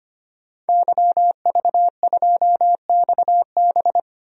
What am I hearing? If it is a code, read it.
Morse code, heard as a keyed tone, about 25 wpm, 700 Hz. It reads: YV2XB
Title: DXCC-100 Call Signs - 5 Characters - Repeated 2x - 25wpm